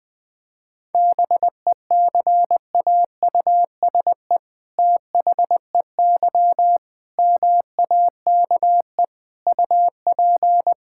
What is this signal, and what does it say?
Morse code, heard as a keyed tone, about 21 wpm, 700 Hz. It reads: BECAUSE THEY MAKE UP
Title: All-Age Appropriate Jokes - 20wpm